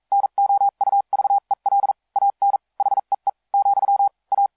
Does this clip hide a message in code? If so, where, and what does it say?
Yes: Morse code from 0.1 to 4.5 s, heard as a keyed tone, about 42 wpm, spaced as for 36 wpm, 800 Hz. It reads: NOUVEL AN HEE , A